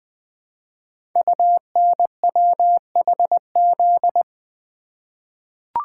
Morse code, heard as a keyed tone, about 20 wpm, 700 Hz. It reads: UNWHZ
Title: Code Groups - 5 Letters - 20wpm